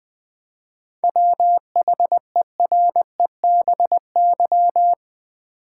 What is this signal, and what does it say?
Morse code, heard as a keyed tone, about 20 wpm, 700 Hz. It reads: WHEREBY
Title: 7 Letter Words - First 3 Letters Given - 20wpm